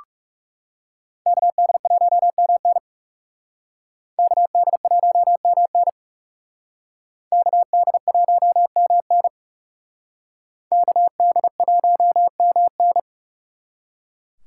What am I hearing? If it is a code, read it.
Morse code, heard as a keyed tone, about 39 wpm, 700 Hz. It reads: KD1MN KD1MN KD1MN KD1MN